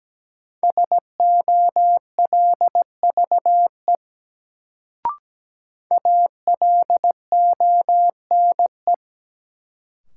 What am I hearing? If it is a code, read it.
Morse code, heard as a keyed tone, about 17 wpm, 700 Hz. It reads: SOLVE ALONE
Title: Sentences from Top 1000 Words - New Word Drill - 17wpm